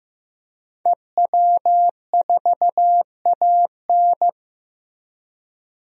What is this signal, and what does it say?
Morse code, heard as a keyed tone, about 15 wpm, 700 Hz. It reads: EW4AN
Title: DXCC-100 Call Signs - 5 Characters - Repeated 3x - 15wpm